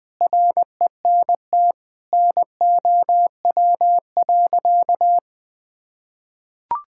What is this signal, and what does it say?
Morse code, heard as a keyed tone, about 20 wpm, 700 Hz. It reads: RENT NOW.
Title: Sentences from Top 1000 Words - Review Part 5 - 20wpm